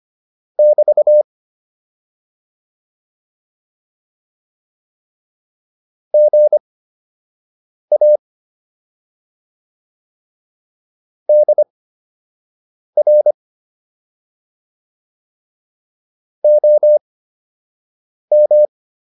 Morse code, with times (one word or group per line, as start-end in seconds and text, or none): 0.6-1.2 <BT>
6.1-8.2 GA
11.3-13.3 DR
16.4-18.6 OM